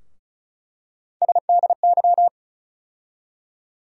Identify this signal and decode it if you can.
Morse code, heard as a keyed tone, about 36 wpm, 700 Hz. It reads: SDY
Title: USA Call Sign Postfixes - 3 Letters 35wpm